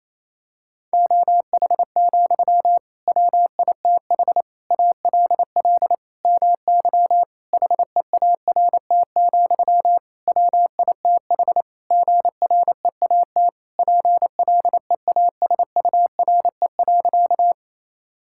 Morse code, heard as a keyed tone, about 28 wpm, 700 Hz. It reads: OH, WITH ALL MY HEART, WITH GREAT PLEASURE.
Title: Sentences from Top 1600 Words 28wpm